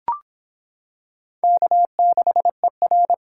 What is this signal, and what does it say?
Morse code, heard as a keyed tone, about 26 wpm, 700 Hz. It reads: K6ER